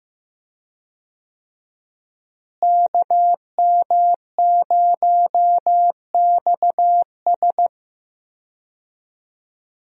Morse code, heard as a keyed tone, about 15 wpm, 700 Hz. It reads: KM0XS
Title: USA Call Signs - 5 Characters - Repeated 2x - 15wpm